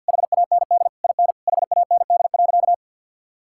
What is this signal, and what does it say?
Morse code, heard as a keyed tone, about 51 wpm, 700 Hz. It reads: HAND IN HAND.
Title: Sentences from Top 1000 Words - Review Part 6 - 50wpm